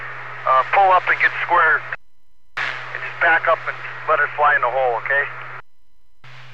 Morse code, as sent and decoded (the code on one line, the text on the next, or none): none